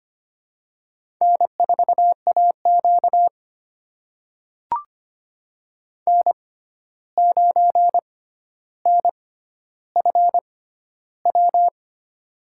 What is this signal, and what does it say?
Morse code, heard as a keyed tone, about 25 wpm, spaced as for 12 wpm, 700 Hz. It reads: N4AQ N9NFW
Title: USA Call Signs - 6x Character Spacing - 25wpm